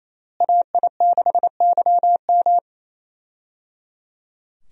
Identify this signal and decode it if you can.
Morse code, heard as a keyed tone, about 28 wpm, 700 Hz. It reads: AI6YM
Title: USA Call Signs - 5 Characters - Repeated 3x - 28wpm